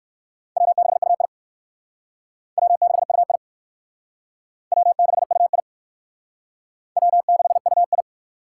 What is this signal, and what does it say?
Morse code, heard as a keyed tone, about 55 wpm, 700 Hz. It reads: W6UI W6UI W6UI W6UI
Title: Speed Racing - USA 1x2 Call Signs 45wpm